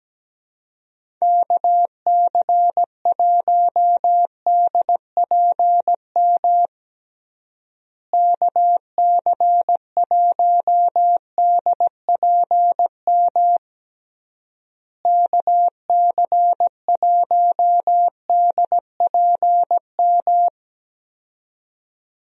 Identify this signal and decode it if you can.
Morse code, heard as a keyed tone, about 17 wpm, 700 Hz. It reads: KC1DPM KC1DPM KC1DPM
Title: USA Call Signs - 6 Characters - Repeated 3x - 17wpm